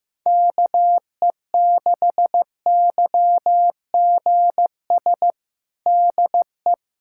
Morse code, heard as a keyed tone, about 15 wpm, 700 Hz. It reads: KE6YGS DE
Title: USA Callsign DE Callsign 15wpm